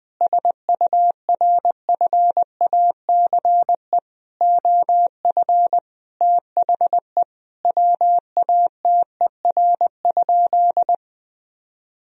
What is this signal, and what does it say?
Morse code, heard as a keyed tone, about 20 wpm, 700 Hz. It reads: SURFACE OF THE WATER?
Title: Sentences from Top 1000 Words - Review Part 3 - 20wpm